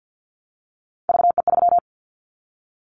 Morse code, heard as a keyed tone, about 50 wpm, 700 Hz. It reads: VE3
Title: Callsign Prefixes for Top 50 DXCC - 50wpm